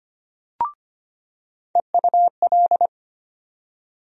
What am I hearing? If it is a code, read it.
Morse code, heard as a keyed tone, about 25 wpm, 700 Hz. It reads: EUL
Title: USA Call Sign Postfixes - 3 Letters 25wpm